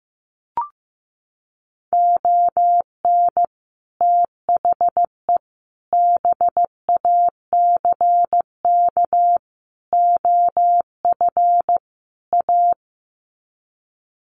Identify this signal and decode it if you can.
Morse code, heard as a keyed tone, about 15 wpm, 700 Hz. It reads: ON THE BACK OF A
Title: Sets of 5 Words Encore2 15wpm